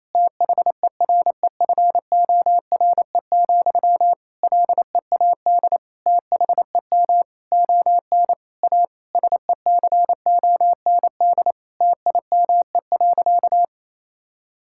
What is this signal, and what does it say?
Morse code, heard as a keyed tone, about 28 wpm, 700 Hz. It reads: THEREFORE, LEAD THEM ON A SECOND TIME.
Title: Sentences from Top 1000 Words - Review Part 5 - 28wpm